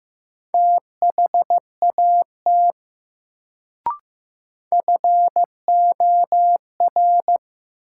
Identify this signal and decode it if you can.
Morse code, heard as a keyed tone, about 16 wpm, 700 Hz. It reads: THAT FOR
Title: Sentences from Top 100 Words - New Word Drill - 15wpm